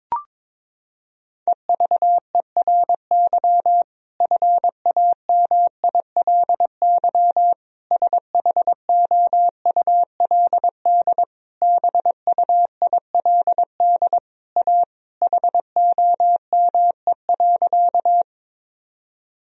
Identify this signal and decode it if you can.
Morse code, heard as a keyed tone, about 22 wpm, 700 Hz. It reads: EVERY FAMILY SHOULD BUILD A HOME.
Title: Sentences from Top 1000 Words - Review Part 1 - 22wpm